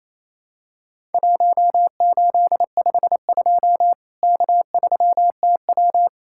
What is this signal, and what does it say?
Morse code, heard as a keyed tone, about 28 wpm, 700 Hz. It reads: 1852 K3TW
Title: Worked All Europe DX Contest - QTC Exchange - 28wpm